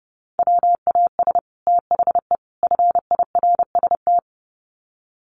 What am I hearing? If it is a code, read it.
Morse code, heard as a keyed tone, about 30 wpm, 700 Hz. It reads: WAS THE FIRST
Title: Sets of 3 Words 30wpm